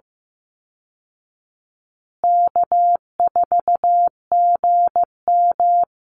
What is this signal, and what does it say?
Morse code, heard as a keyed tone, about 15 wpm, 700 Hz. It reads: K4GM
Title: Easy Call Signs Repeated 3x - 15wpm